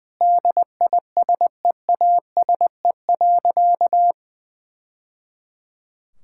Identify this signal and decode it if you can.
Morse code, heard as a keyed tone, about 20 wpm, 700 Hz. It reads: DISEASE.
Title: Sentences from Top 1000 Words - Review Part 6 - 20wpm